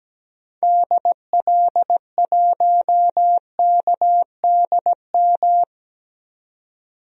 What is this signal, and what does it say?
Morse code, heard as a keyed tone, about 17 wpm, 700 Hz. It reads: DL1KDM
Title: DXCC-100 Call Signs - 6 Characters - 17wpm